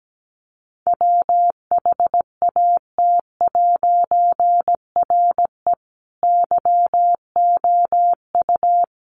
Morse code, heard as a keyed tone, about 17 wpm, 700 Hz. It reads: WHAT'RE YOU
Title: A Sound Of Thunder 17wpm